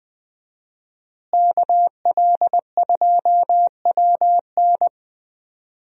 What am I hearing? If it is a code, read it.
Morse code, heard as a keyed tone, about 19 wpm, 700 Hz. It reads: KL2WN
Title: USA Call Signs - 5 Characters - Repeated 3x - 20wpm